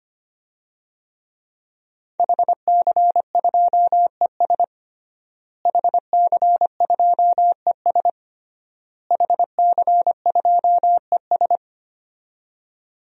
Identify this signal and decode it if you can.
Morse code, heard as a keyed tone, about 25 wpm, 700 Hz. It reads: HC2ES HC2ES HC2ES
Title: DXCC-100 Call Signs - 5 Characters - Repeated 3x - 25wpm